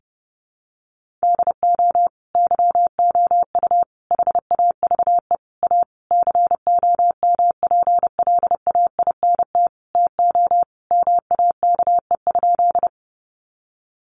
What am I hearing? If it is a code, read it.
Morse code, heard as a keyed tone, about 30 wpm, 700 Hz. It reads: DO YOU HAVE A COMPLAINT TO MAKE?
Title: Sentences from Top 1400 Words 30wpm